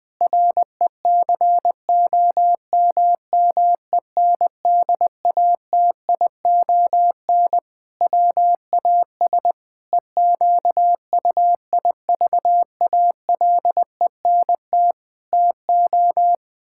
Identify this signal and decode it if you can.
Morse code, heard as a keyed tone, about 21 wpm, 700 Hz. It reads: RECOMMENDATION WAS EQUIVALENT TO
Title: Sentences from Top 1600 Words 20wpm